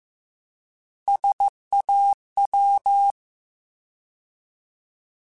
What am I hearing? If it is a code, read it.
Morse code, heard as a keyed tone, about 15 wpm, 800 Hz. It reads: SAW
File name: Alphabetical / Alphabetical mp3s / Ss @13